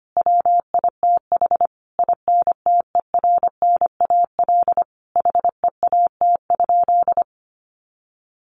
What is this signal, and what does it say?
Morse code, heard as a keyed tone, about 25 wpm, 700 Hz. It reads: WITH INTERNAL HEAT?
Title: Sentences from Top 1400 Words 25wpm